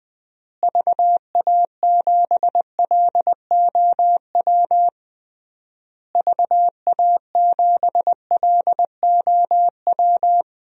Morse code, heard as a keyed tone, about 21 wpm, 700 Hz. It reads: VA7LOW VA7LOW
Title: DXCC-100 Call Signs - 6 Characters - Repeated 3x - 20wpm